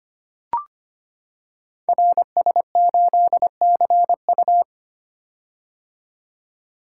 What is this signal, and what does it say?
Morse code, heard as a keyed tone, about 25 wpm, 700 Hz. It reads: RS8CU
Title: Code Groups - 5 Letter:Numbers - 25wpm